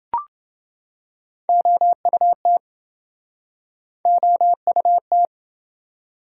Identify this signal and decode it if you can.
Morse code, heard as a keyed tone, about 28 wpm, 700 Hz. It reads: OUT OUT